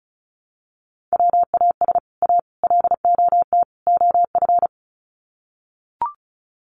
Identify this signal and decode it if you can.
Morse code, heard as a keyed tone, about 34 wpm, 700 Hz. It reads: WAS A LOT OF
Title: Sets of 4 Words 35wpm